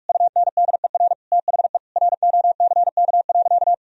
Morse code, heard as a keyed tone, about 41 wpm, 700 Hz. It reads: UNDER THE ROCK.
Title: Sentences from Top 1000 Words - Review Part 2 - 45wpm